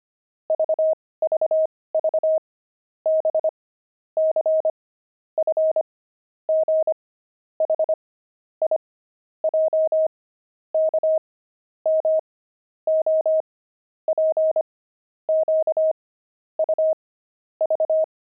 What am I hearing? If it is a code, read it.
Morse code, heard as a keyed tone, about 25 wpm, spaced as for 18 wpm, 630 Hz. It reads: VVV B C F G H I J K M O P Q U V
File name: pangram-fs-18-cs-25-letters